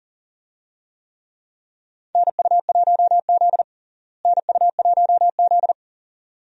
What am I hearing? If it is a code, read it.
Morse code, heard as a keyed tone, about 40 wpm, 700 Hz. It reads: NU1Z NU1Z
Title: USA Call Signs - 4 Characters - Repeated 2x - 40wpm